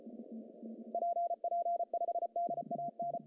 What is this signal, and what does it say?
Morse code, heard as a keyed tone, about 34 wpm, 690 Hz. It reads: PP5DAN
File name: CW / cqww-cw-k5tr